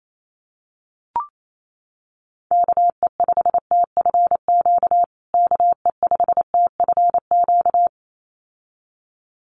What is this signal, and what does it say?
Morse code, heard as a keyed tone, about 28 wpm, 700 Hz. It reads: KE5TFQ KE5TFQ
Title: USA Call Signs Repeated 2x 28wpm